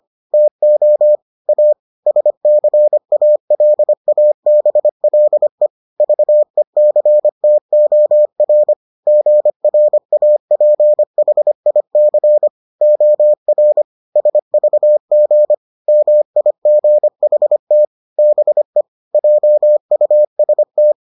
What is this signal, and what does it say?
Morse code, heard as a keyed tone, about 25 wpm, 600 Hz. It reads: TO A SCALABLE VECTOR GRAPHIC OR SVG MIGHT BE JUST